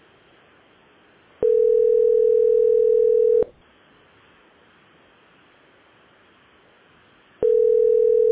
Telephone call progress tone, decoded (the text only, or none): ringback tone